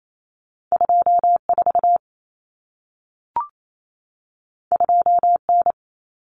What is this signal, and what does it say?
Morse code, heard as a keyed tone, about 28 wpm, 700 Hz. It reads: 24 2N
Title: Common Cut Numbers - 2-Digit 28wpm